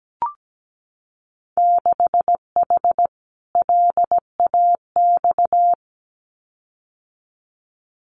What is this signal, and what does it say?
Morse code, heard as a keyed tone, about 17 wpm, 700 Hz. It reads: 6H LAX
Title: Winter Field Day Exchanges Encore 17wpm